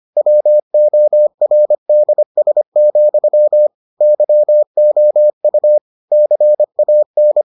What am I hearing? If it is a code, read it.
Morse code, heard as a keyed tone, about 25 wpm, 600 Hz. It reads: WORDS, YOU CAN